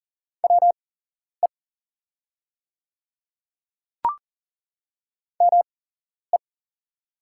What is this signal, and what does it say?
Morse code, heard as a keyed tone, about 41 wpm, spaced as for 11 wpm, 700 Hz. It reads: WE ME
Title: 8x Character Spacing - Top 200 Words - 40wpm